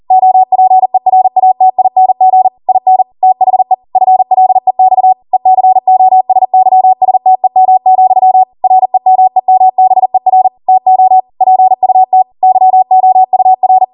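Morse code, heard as a keyed tone, about 40 wpm, 750 Hz. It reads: OPERATING IN THE FLEX ECOSYSTEM, REMEMBER TO PUT YOUR
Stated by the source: Code practice at 40 wpm.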